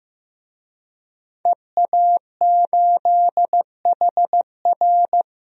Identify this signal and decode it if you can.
Morse code, heard as a keyed tone, about 15 wpm, 700 Hz. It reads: EA8HR
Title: DXCC-100 Call Signs - 5 Characters - 15wpm